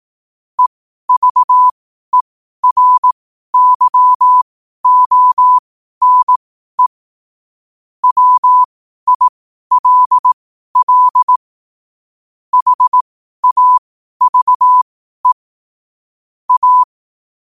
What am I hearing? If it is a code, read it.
Morse code, heard as a keyed tone, about 18 wpm, spaced as for 12 wpm, 1000 Hz. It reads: EVERYONE WILL HAVE A
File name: qotdfiles / 12wpm